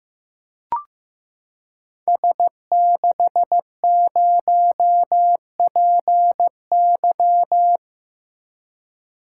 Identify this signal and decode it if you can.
Morse code, heard as a keyed tone, about 15 wpm, 700 Hz. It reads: S60PY
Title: Code Groups - 5 Letter:Numbers - 15wpm